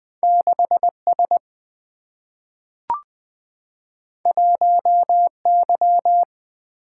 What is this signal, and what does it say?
Morse code, heard as a keyed tone, about 20 wpm, 700 Hz. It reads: 6 S 1 Y
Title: Single-Letter Single-Number 20wpm